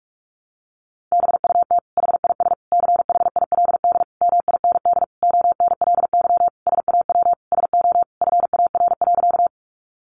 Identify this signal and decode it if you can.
Morse code, heard as a keyed tone, about 44 wpm, 700 Hz. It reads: BUT HIS CHILD MIND ONLY SAW SO FAR.
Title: Sentences from Top 800 Words 45wpm